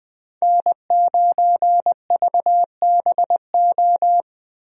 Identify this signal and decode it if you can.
Morse code, heard as a keyed tone, about 20 wpm, 700 Hz. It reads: N9VBO
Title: USA Call Signs - 5 Characters - Repeated 3x - 20wpm